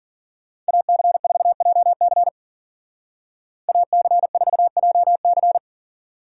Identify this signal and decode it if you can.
Morse code, heard as a keyed tone, about 53 wpm, 700 Hz. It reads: AC4JC AC4JC